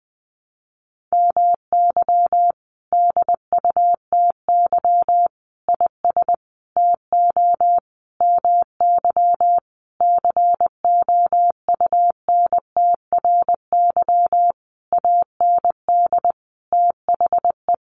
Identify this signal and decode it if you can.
Morse code, heard as a keyed tone, about 20 wpm, 700 Hz. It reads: MY DUTY IS TO MY COUNTRY AND THE
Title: Sentences from Top 1500 Words 20wpm